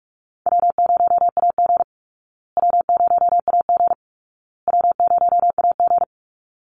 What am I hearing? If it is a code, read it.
Morse code, heard as a keyed tone, about 44 wpm, 700 Hz. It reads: W0AG W0AG W0AG
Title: Easy Call Signs Repeated 3x - 45wpm